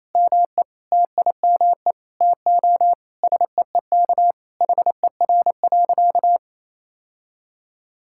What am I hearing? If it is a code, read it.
Morse code, heard as a keyed tone, about 28 wpm, 700 Hz. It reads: ME TIME TO SEEK HER.
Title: Sentences from Top 1600 Words 28wpm